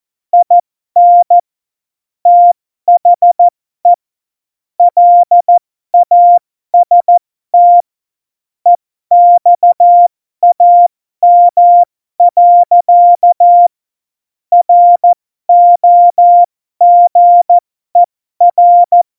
Morse code, heard as a keyed tone, about 14 wpm, spaced as for 12 wpm, 700 Hz. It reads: IN THE LAST EXAM. ROGER
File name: Morsekurs / Witze